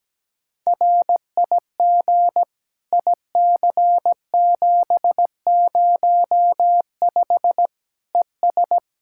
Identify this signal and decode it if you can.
Morse code, heard as a keyed tone, about 17 wpm, 700 Hz. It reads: RIG IC705 ES
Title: Practice QSO - Exchange 2 - Info - 17wpm